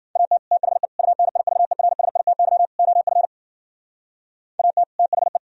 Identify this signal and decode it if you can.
Morse code, heard as a keyed tone, about 49 wpm, 700 Hz. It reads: AT THE UNIVERSITY OF AT THE
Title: Speed-Racing - Sets of 4 Words 40wpm